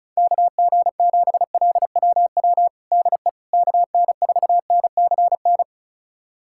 Morse code, heard as a keyed tone, about 36 wpm, 700 Hz. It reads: KG7LWW DE KN4NCN
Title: USA Callsign DE Callsign 35wpm